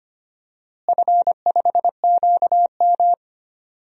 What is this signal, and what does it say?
Morse code, heard as a keyed tone, about 25 wpm, 700 Hz. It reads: F5QM
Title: DXCC-100 Call Signs - 4 Characters - Repeated 3x - 25wpm